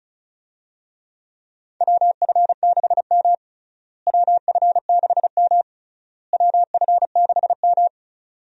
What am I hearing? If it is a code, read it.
Morse code, heard as a keyed tone, about 34 wpm, 700 Hz. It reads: WF6M WF6M WF6M
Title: USA Call Signs - 4 Characters - Repeated 3x - 35wpm